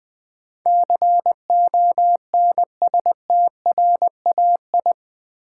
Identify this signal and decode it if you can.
Morse code, heard as a keyed tone, about 21 wpm, 700 Hz. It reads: CONSTRAI
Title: Build A Word - 4 Letters At A Time - 10-Letter Words - 20wpm